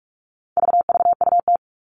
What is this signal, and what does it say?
Morse code, heard as a keyed tone, about 44 wpm, 700 Hz. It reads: VVUT
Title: All Cut Numbers - 4-Digit 45wpm